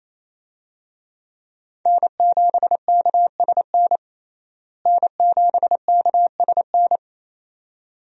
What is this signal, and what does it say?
Morse code, heard as a keyed tone, about 28 wpm, 700 Hz. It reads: N7KSN N7KSN
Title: USA Call Signs - 5 Characters - Repeated 2x - 28wpm